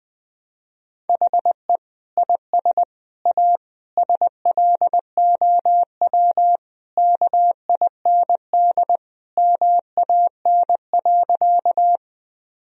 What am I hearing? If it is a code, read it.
Morse code, heard as a keyed tone, about 20 wpm, 700 Hz. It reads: HE IS A SLOW KIND MAN.
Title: Sentences from Top 1000 Words - Review Part 2 - 20wpm